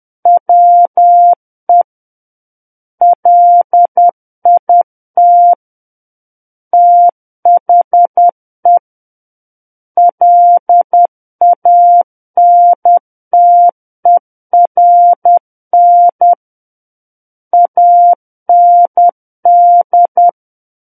Morse code, heard as a keyed tone, about 10 wpm, 700 Hz. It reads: WE LIT THE LANTERN AND
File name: Book One / 01-Chapter 1 - 10 WPM